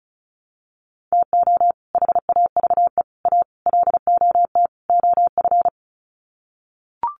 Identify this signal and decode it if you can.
Morse code, heard as a keyed tone, about 34 wpm, 700 Hz. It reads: TO HAVE A LOT OF
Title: Sets of 5 Words Encore 35wpm